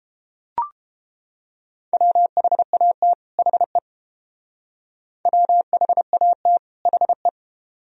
Morse code, heard as a keyed tone, about 31 wpm, 700 Hz. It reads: WHAT HE WHAT HE